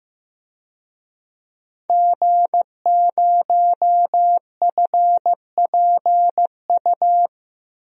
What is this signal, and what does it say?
Morse code, heard as a keyed tone, about 15 wpm, 700 Hz. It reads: G0FPU
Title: DXCC-100 Call Signs - 5 Characters - 15wpm